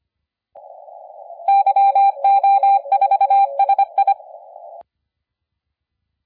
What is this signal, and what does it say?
Morse code, heard as a keyed tone, about 25 wpm, 790 Hz. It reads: YO4SI
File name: la4c / sac-cw-2017 / CONTEST AUDIO 1100Z